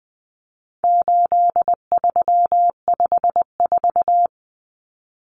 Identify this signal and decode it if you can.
Morse code, heard as a keyed tone, about 20 wpm, 700 Hz. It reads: 8354
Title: Numbers - 4-Digit 20wpm